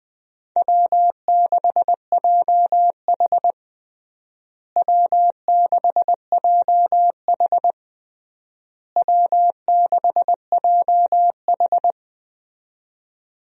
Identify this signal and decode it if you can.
Morse code, heard as a keyed tone, about 20 wpm, 700 Hz. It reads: W6JH W6JH W6JH
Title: USA Call Signs - 4 Characters - Repeated 3x - 20wpm